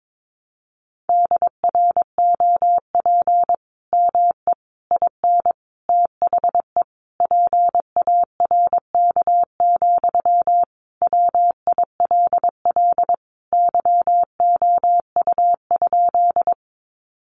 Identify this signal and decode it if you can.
Morse code, heard as a keyed tone, about 22 wpm, 700 Hz. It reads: DROP ME IN THE PARK, WILL YOU?
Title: Sentences from Top 1400 Words 22wpm